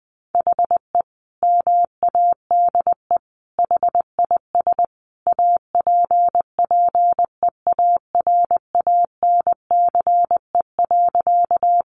Morse code, heard as a keyed tone, about 19 wpm, 700 Hz. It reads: HE MADE HIS APPEARANCE.
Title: Sentences from Top 1100 Words 20wpm